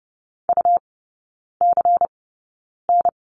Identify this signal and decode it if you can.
Morse code, heard as a keyed tone, about 30 wpm, spaced as for 9 wpm, 700 Hz. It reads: UCN
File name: Lesson 18e - Letter m - Three Characters 30wpm